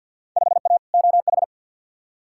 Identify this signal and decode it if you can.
Morse code, heard as a keyed tone, about 53 wpm, 700 Hz. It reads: 5A OH
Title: Field Day Exchanges Encore - 50wpm